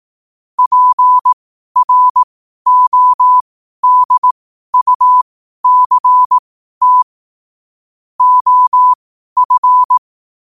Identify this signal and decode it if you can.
Morse code, heard as a keyed tone, about 18 wpm, spaced as for 12 wpm, 1000 Hz. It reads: PRODUCT OF
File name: qotdfiles / 12wpm